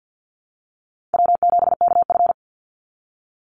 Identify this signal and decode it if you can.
Morse code, heard as a keyed tone, about 50 wpm, 700 Hz. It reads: W7KF
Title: Call Signs - 50wpm